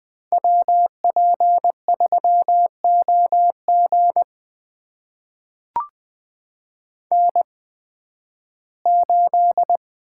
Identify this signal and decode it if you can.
Morse code, heard as a keyed tone, about 20 wpm, spaced as for 13 wpm, 700 Hz. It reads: WP3OG N8
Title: USA Call Signs - 5 Characters - 8x Character Spacing - 20wpm